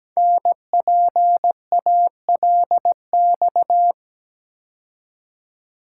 Morse code, heard as a keyed tone, about 17 wpm, 700 Hz. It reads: NPALX
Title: Code Groups - 5 Letter:Numbers - 17wpm